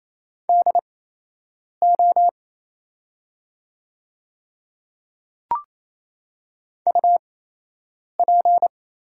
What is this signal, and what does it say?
Morse code, heard as a keyed tone, about 28 wpm, spaced as for 8 wpm, 700 Hz. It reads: DO UP
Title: 8x Character Spacing - Top 100 Words - 28wpm